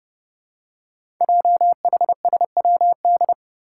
Morse code, heard as a keyed tone, about 30 wpm, 700 Hz. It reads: JHSWD
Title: Code Groups - 5 Letters - 30wpm